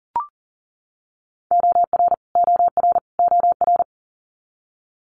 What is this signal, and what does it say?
Morse code, heard as a keyed tone, about 40 wpm, 700 Hz. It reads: OR OR OR
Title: Multi-letter Phonemes Repeated 3x 40wpm